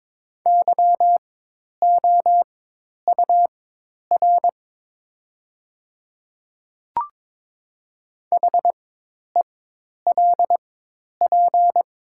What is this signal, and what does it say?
Morse code, heard as a keyed tone, about 21 wpm, spaced as for 10 wpm, 700 Hz. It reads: YOUR HELP
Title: 4x Character Spacing - Top 500 Words - 22wpm